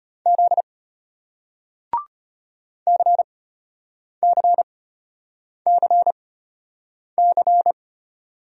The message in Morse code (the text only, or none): Z CCCC